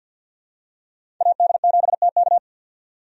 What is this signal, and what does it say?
Morse code, heard as a keyed tone, about 53 wpm, 700 Hz. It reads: AD7TK
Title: USA Call Signs - 5 Characters - 50wpm